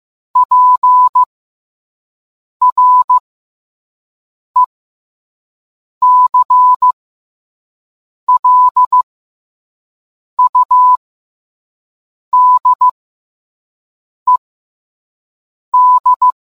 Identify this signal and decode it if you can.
Morse code, heard as a keyed tone, about 15 wpm, spaced as for 5 wpm, 1000 Hz. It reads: PRECLUDED